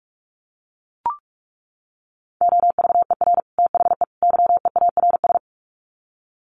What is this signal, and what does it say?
Morse code, heard as a keyed tone, about 44 wpm, 700 Hz. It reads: OVER THE YEARS
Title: Sets of 3 Words 45wpm